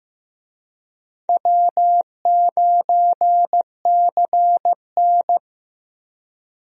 Morse code, heard as a keyed tone, about 15 wpm, 700 Hz. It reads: W9CN
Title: USA Call Signs - 4 Characters - 15wpm